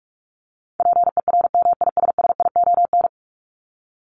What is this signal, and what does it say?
Morse code, heard as a keyed tone, about 44 wpm, 700 Hz. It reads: PERMISSION
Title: Top 1501-1750 Words 45wpm